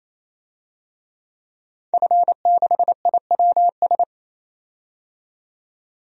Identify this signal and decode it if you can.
Morse code, heard as a keyed tone, about 28 wpm, 700 Hz. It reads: F6IWS
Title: DXCC-100 Call Signs - 5 Characters - 28wpm